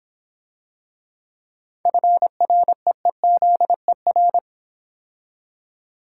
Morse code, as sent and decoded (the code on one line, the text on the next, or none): ..-. .-. . --.. .-.
FREEZER